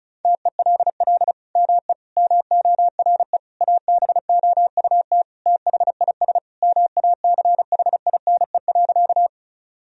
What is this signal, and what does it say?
Morse code, heard as a keyed tone, about 34 wpm, 700 Hz. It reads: TELL ME MORE ABOUT THIS MACHINE.
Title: Sentences from Top 1000 Words - Review Part 2 - 35wpm